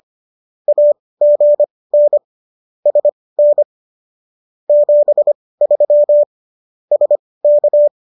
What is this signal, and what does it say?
Morse code, heard as a keyed tone, about 25 wpm, spaced as for 18 wpm, 590 Hz. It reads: AGN SN 73 SK